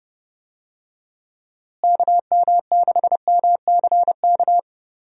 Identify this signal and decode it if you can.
Morse code, heard as a keyed tone, about 31 wpm, 700 Hz. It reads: KM6MCK
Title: USA Call Signs - 6 Characters - Repeated 2x - 30wpm